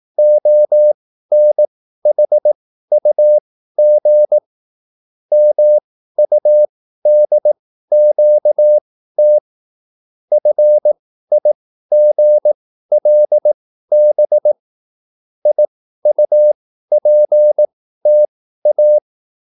Morse code, heard as a keyed tone, about 18 wpm, spaced as for 13 wpm, 600 Hz. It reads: ONHUG MUDQT FIGLB IUPTA